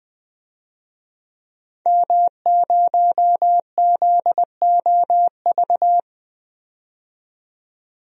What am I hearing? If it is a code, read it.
Morse code, heard as a keyed tone, about 20 wpm, 700 Hz. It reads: M0ZOV
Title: DXCC-100 Call Signs - 5 Characters - 20wpm